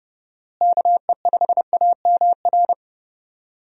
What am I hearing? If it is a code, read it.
Morse code, heard as a keyed tone, about 32 wpm, 700 Hz. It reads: KE5AMR
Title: USA Call Signs - 6 Characters - Repeated 3x - 30wpm